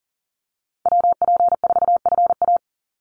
Morse code, heard as a keyed tone, about 38 wpm, 700 Hz. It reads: WP4FA
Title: USA Call Signs 40wpm